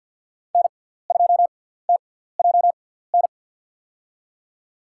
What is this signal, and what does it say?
Morse code, heard as a keyed tone, about 53 wpm, spaced as for 17 wpm, 700 Hz. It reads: N2TJN
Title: USA Call Signs - 5 Characters - 6x Character Spacing - 50wpm